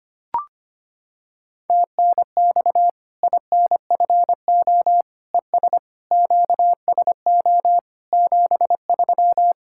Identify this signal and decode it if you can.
Morse code, heard as a keyed tone, about 25 wpm, 700 Hz. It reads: TNX INFO ES QSO 73
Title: K1USN SST QSO Practice 25wpm